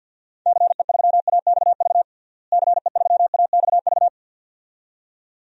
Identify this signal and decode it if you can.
Morse code, heard as a keyed tone, about 52 wpm, 700 Hz. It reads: CE3AXV CE3AXV
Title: DXCC-100 Call Signs - 6 Characters - Repeated 2x - 50wpm